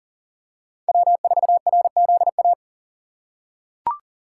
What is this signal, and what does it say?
Morse code, heard as a keyed tone, about 40 wpm, 690 Hz. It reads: W4RZA
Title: USA Call Signs - 5 Characters - 6x Character Spacing - 40wpm